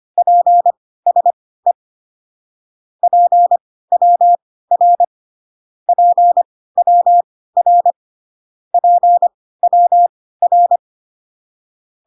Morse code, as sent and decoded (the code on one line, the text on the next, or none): .--. ... / .--. .-- .-. / .--. .-- .-. / .--. .-- .-.
PSE PWR PWR PWR